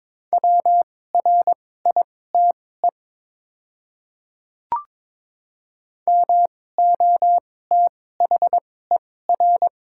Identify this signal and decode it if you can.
Morse code, heard as a keyed tone, about 22 wpm, spaced as for 16 wpm, 700 Hz. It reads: WRITE MOTHER
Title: 2x Character Spacing - Top 200 Words - 22wpm